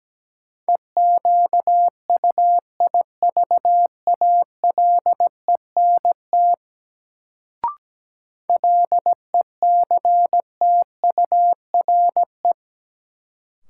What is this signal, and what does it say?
Morse code, heard as a keyed tone, about 17 wpm, 700 Hz. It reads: EQUIVALENT LECTURE
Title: Sentences from Top 1600 Words - New Word Drill - 17wpm